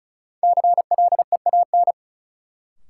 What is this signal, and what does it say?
Morse code, heard as a keyed tone, about 34 wpm, 700 Hz. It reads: CLEAN
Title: Sentences from Top 1100 Words - New Word Drill - 35wpm